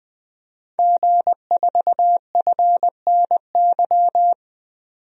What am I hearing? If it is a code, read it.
Morse code, heard as a keyed tone, about 20 wpm, 700 Hz. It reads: G4FNY
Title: DXCC-100 Call Signs - 5 Characters - 20wpm